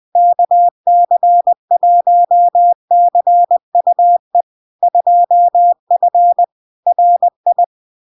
Morse code, heard as a keyed tone, about 20 wpm, 700 Hz. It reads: KC1CUE 2F RI
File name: LARK-FD-2022-Log-in-CW-20wpm-700-Hz